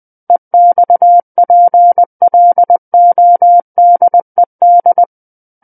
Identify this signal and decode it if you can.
Morse code, heard as a keyed tone, about 20 wpm, 700 Hz. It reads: EXPLODED
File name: 20 WPM